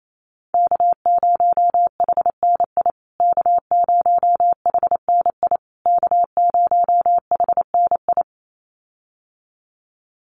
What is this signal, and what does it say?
Morse code, heard as a keyed tone, about 28 wpm, 700 Hz. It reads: K0HNI K0HNI K0HNI
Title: USA Call Signs Repeated 3x 28wpm